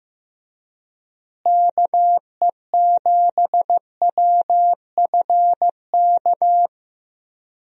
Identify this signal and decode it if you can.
Morse code, heard as a keyed tone, about 15 wpm, 700 Hz. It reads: KE7WFK
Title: USA Call Signs - 6 Characters - Repeated 2x - 15wpm